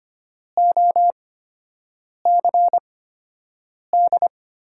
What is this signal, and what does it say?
Morse code, heard as a keyed tone, about 25 wpm, 700 Hz. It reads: O C D
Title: USA Call Signs - 8x Character Spacing - 25wpm